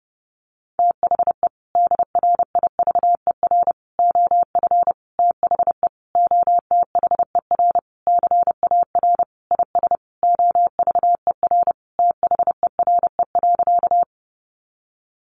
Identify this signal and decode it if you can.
Morse code, heard as a keyed tone, about 30 wpm, 700 Hz. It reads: THE DRIVER OF THE OTHER CAR IS OVER THERE.
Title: Sentences from Top 1000 Words 30wpm